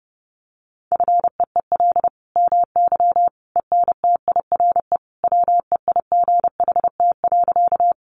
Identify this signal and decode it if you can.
Morse code, heard as a keyed tone, about 30 wpm, 700 Hz. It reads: FEEL MY ENTIRE WEIGHT.
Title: Sentences from Top 800 Words 30wpm